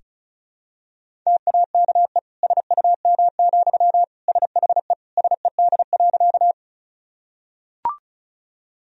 Morse code, heard as a keyed tone, about 36 wpm, 700 Hz. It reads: TAKE SUM, SHE SED.
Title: Sentences from Top 200 Words and Abbreviations 35wpm